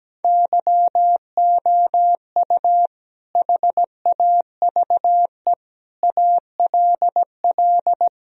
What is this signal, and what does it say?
Morse code, heard as a keyed tone, about 17 wpm, 700 Hz. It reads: YOU HAVE ALL
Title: Sentences from Top 1000 Words - Review Part 2 - 17wpm